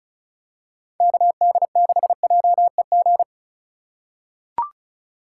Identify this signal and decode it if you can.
Morse code, heard as a keyed tone, about 34 wpm, 700 Hz. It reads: KD6JEG
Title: USA Call Signs - 6 Characters - 2x Character Spacing - 35wpm